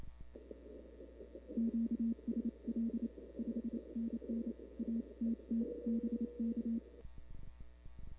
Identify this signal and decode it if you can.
Morse code, heard as a keyed tone, about 28 wpm, 230 Hz. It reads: QSL 5NN ATT BK
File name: CW / 2005-k5tr arrl cw dx -00:00